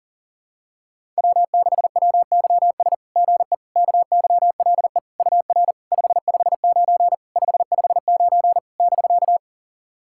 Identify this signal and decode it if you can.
Morse code, heard as a keyed tone, about 40 wpm, 690 Hz. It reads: W6WYS GE KYLE UR 559 559 <BK>
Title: SOTA QSO Practice 40wpm